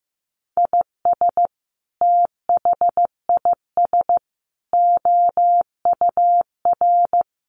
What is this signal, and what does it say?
Morse code, heard as a keyed tone, about 15 wpm, 700 Hz. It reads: IS THIS OUR
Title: Sentences from Top 1500 Words 15wpm